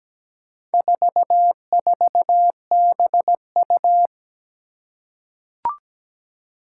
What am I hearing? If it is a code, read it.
Morse code, heard as a keyed tone, about 17 wpm, 700 Hz. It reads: 44BU
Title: Four Challenging Letter-Number Combinations 17wpm